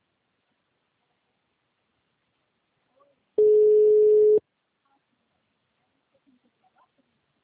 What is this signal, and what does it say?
Ringback tone (Europe, 425 Hz).
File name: out- -2005- - - .77825